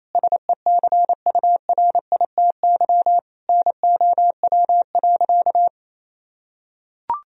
Morse code, heard as a keyed tone, about 28 wpm, 700 Hz. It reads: SECURITY NOW.
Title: Sentences from Top 1000 Words - Review Part 4 - 28wpm